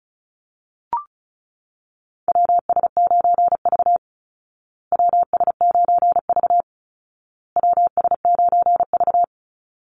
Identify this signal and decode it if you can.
Morse code, heard as a keyed tone, about 34 wpm, 700 Hz. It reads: WS9V WS9V WS9V
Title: Call Signs Repeated 2x - 35wpm